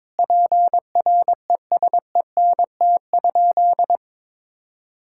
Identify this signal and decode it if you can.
Morse code, heard as a keyed tone, about 22 wpm, 700 Hz. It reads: PRESENT?
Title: Sentences from Top 1000 Words - Review Part 5 - 22wpm